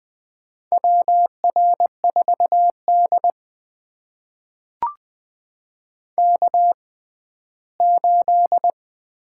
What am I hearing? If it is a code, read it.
Morse code, heard as a keyed tone, about 20 wpm, spaced as for 14 wpm, 700 Hz. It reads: WR4D K8
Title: USA Call Signs - 4 Characters - 6x Character Spacing - 20wpm